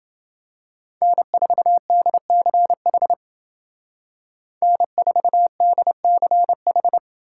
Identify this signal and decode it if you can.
Morse code, heard as a keyed tone, about 28 wpm, 700 Hz. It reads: N4DCH N4DCH